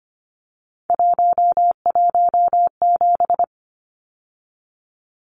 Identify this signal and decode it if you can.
Morse code, heard as a keyed tone, about 25 wpm, 690 Hz. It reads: 117
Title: Numbers - 3-Digit 25wpm